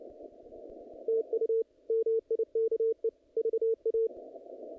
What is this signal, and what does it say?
Morse code, heard as a keyed tone, about 29 wpm, 450 Hz. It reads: TU MIKE VA
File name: contests / naqp-cw-k5tr-op-k5pi- -2300